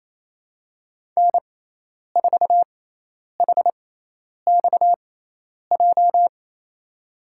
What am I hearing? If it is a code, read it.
Morse code, heard as a keyed tone, about 28 wpm, spaced as for 10 wpm, 700 Hz. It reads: N4HXJ
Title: USA Call Signs - 5 Characters - 6x Character Spacing - 28wpm